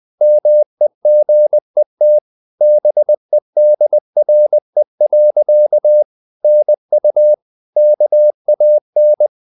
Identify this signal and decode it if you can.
Morse code, heard as a keyed tone, about 20 wpm, 600 Hz. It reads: MEGET BEDRE. NU KAN
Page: Ferie 20 wpm